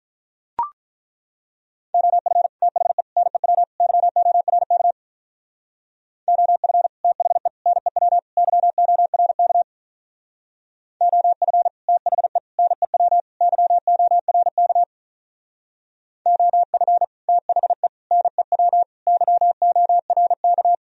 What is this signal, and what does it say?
Morse code, heard as a keyed tone, about 42 wpm, 700 Hz. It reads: OF THE NEW YORK OF THE NEW YORK OF THE NEW YORK OF THE NEW YORK
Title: Speed-Racing - Sets of 4 Words 35wpm